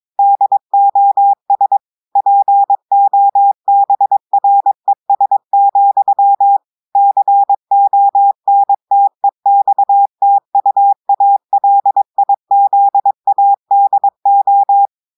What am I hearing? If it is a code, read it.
Morse code, heard as a keyed tone, about 23 wpm, 800 Hz. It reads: DOS POBRES, CONTEXTUALIZADO
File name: audios / portugais / portugais texte en clair 1 vitesse 22 mots minutes